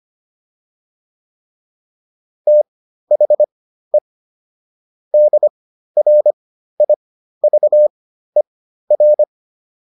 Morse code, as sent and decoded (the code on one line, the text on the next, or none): - .... / -.. .-. .. ...- . .-.
THE DRIVER